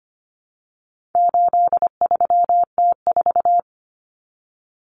Morse code, heard as a keyed tone, about 25 wpm, 700 Hz. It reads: 83T4
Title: Common Cut Numbers - 4-Digit 25wpm